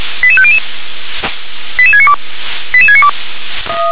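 Five-tone selective call (ZVEI), heard as quick runs of five tones, tones of about 70 ms. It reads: 88500 88692 88692
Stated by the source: ZVEI five-tone call received by SDR.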